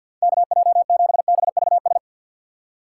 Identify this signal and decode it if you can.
Morse code, heard as a keyed tone, about 48 wpm, 700 Hz. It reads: KJ7BVS
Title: USA Call Signs - 6 Characters - 4x Character Spacing - 50wpm